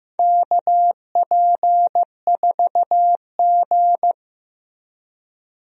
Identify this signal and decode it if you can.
Morse code, heard as a keyed tone, about 15 wpm, 700 Hz. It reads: KP4G
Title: DXCC-100 Call Signs - 4 Characters - Repeated 3x - 15wpm